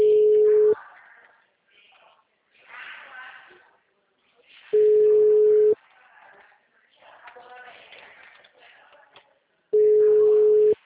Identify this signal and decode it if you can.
Ringback tone (Europe, 425 Hz), with speech.